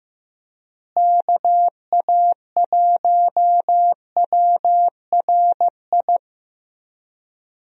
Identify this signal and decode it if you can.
Morse code, heard as a keyed tone, about 15 wpm, 700 Hz. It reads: KA1WRI
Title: USA Call Signs - 6 Characters - Repeated 2x - 15wpm